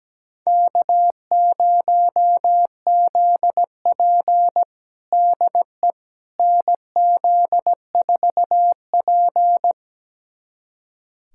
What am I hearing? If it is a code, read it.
Morse code, heard as a keyed tone, about 17 wpm, 700 Hz. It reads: K0ZP DE NZ4P
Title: USA Callsign DE Callsign - 4 Characters - 17wpm